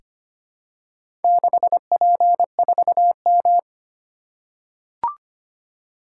Morse code, heard as a keyed tone, about 25 wpm, 700 Hz. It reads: 6P4M
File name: practice / 25wpm / Four Letter-Number Combinations - 25wpm